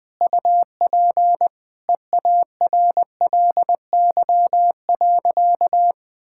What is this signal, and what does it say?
Morse code, heard as a keyed tone, about 20 wpm, 700 Hz. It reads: UP EARLY.
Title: Sentences with 5-Word Phrases - Spoken Beginning - 20wpm